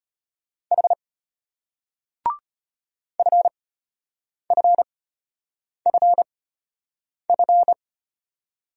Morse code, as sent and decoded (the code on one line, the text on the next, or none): .... / ..-. ..-. ..-. ..-.
H FFFF